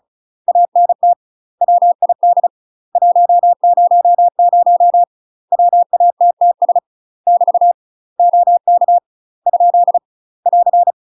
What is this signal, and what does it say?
Morse code, heard as a keyed tone, about 36 wpm, 700 Hz. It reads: ANT WID 100 WATTS <BT> OK ? <AR>
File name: UK MASTER QSOs TRAINING/35WPM